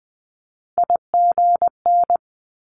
Morse code, heard as a keyed tone, about 20 wpm, 700 Hz. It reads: IGN
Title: Multi-letter Phonemes Quick Fire 20wpm